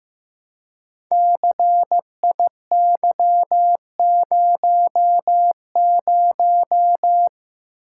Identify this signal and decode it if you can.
Morse code, heard as a keyed tone, about 15 wpm, 700 Hz. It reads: CIY00
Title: Code Groups - 5 Letter:Numbers - 15wpm